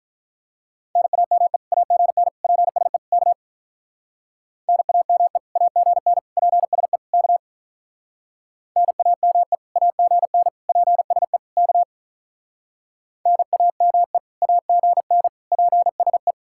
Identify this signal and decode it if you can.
Morse code, heard as a keyed tone, about 43 wpm, 700 Hz. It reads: NAME AGN PSE K NAME AGN PSE K NAME AGN PSE K NAME AGN PSE
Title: Speed-Racing Common QSO Phrases - 35wpm